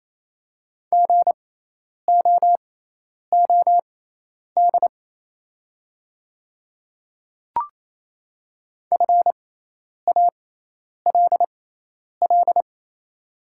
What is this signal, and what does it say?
Morse code, heard as a keyed tone, about 28 wpm, spaced as for 10 wpm, 700 Hz. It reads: GOOD FALL
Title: 6x Character Spacing - Top 500 Words - 28wpm